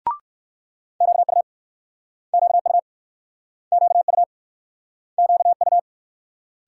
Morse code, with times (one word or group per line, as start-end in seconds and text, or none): 1.0-1.4 QU
2.3-2.8 QU
3.7-4.2 QU
5.2-5.8 QU